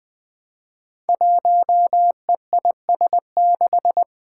Morse code, heard as a keyed tone, about 20 wpm, 700 Hz. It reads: 1EIS6
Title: Five Challenging Letter-Number Combinations 20wpm